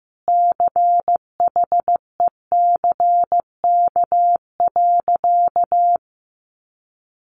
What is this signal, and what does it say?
Morse code, heard as a keyed tone, about 15 wpm, 700 Hz. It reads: CHECK.
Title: A Sound Of Thunder 15wpm